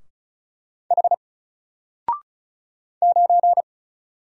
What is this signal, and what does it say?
Morse code, heard as a keyed tone, about 34 wpm, 700 Hz. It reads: H 9
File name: Single-Letter-Number-35wpm